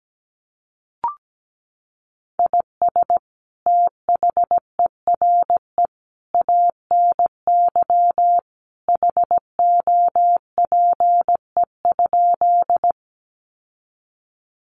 Morse code, heard as a keyed tone, about 17 wpm, 700 Hz. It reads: IS THERE ANY HOPE?
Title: Sentences from Top 1000 Words 17wpm